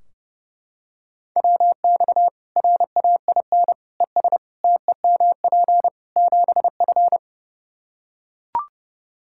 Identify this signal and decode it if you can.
Morse code, heard as a keyed tone, about 32 wpm, 700 Hz. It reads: WX RAIN ES TEMP 7F
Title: Practice QSO - Exchange 2 - Info - 30wpm